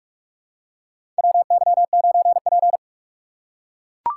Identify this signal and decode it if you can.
Morse code, heard as a keyed tone, about 42 wpm, 700 Hz. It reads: WY9P
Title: USA Call Signs - 4 Characters - 6x Character Spacing - 45wpm